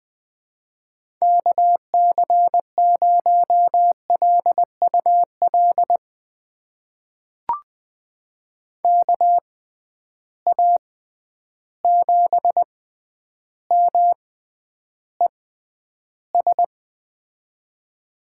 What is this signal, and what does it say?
Morse code, heard as a keyed tone, about 20 wpm, spaced as for 10 wpm, 700 Hz. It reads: KC0LUL KA7MES
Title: USA Call Signs - 6 Characters - 6x Character Spacing - 20wpm